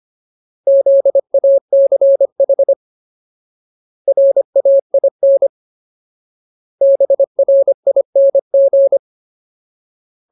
Morse code, heard as a keyed tone, about 25 wpm, 550 Hz. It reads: ZACH RAIN BRING
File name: Ultimate Word List / 25WPM Ultimate